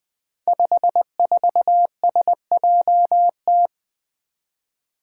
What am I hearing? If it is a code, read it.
Morse code, heard as a keyed tone, about 20 wpm, 700 Hz. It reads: 54SJT
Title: Five Challenging Letter-Number Combinations 20wpm